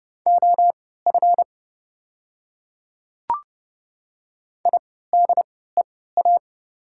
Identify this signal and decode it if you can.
Morse code, heard as a keyed tone, about 30 wpm, spaced as for 17 wpm, 690 Hz. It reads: OF IDEA
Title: 3x Character Spacing - Top 300 Words - 30wpm